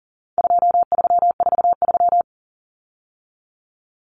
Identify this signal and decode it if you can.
Morse code, heard as a keyed tone, about 40 wpm, 700 Hz. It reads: 2343
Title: Numbers - 4-Digit 40wpm